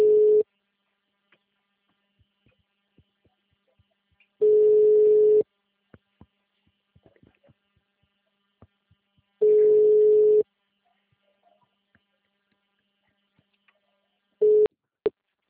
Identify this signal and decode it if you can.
Ringback tone (Europe, 425 Hz).